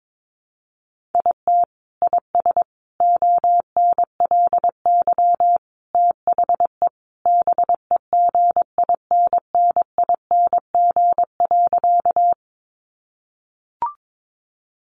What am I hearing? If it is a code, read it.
Morse code, heard as a keyed tone, about 22 wpm, 700 Hz. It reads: IT IS ONLY THE BEGINNING.
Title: Sentences from Top 800 Words 22wpm